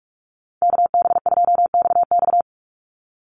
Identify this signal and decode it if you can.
Morse code, heard as a keyed tone, about 44 wpm, 700 Hz. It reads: KB2XX
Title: NA SOTA Chaser Callsigns 45wpm